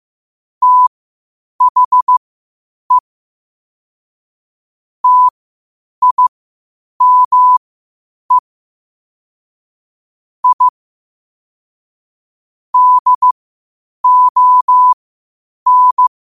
Morse code, heard as a keyed tone, about 15 wpm, spaced as for 8 wpm, 1000 Hz. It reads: THE TIME I DON
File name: qotdfiles / 8wpm